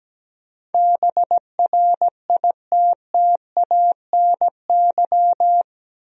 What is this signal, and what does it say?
Morse code, heard as a keyed tone, about 17 wpm, 700 Hz. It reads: BRITTANY
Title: USA Common Names - 17wpm